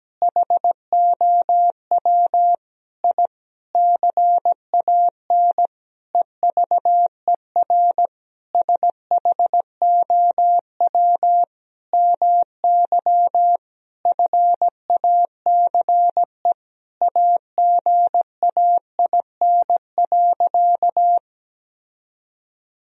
Morse code, heard as a keyed tone, about 17 wpm, 700 Hz. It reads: HOW I CAN EVER SHOW MY FACE AGAIN.
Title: Sentences from Top 1000 Words - Review Part 2 - 17wpm